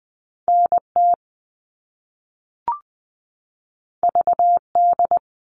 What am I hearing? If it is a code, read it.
Morse code, heard as a keyed tone, about 20 wpm, 700 Hz. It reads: NT VD
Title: All Cut Numbers - 2-Digit 20wpm